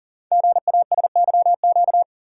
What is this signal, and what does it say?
Morse code, heard as a keyed tone, about 40 wpm, 700 Hz. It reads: GASYQ
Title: Code Groups - 5 Letter:Numbers - 40wpm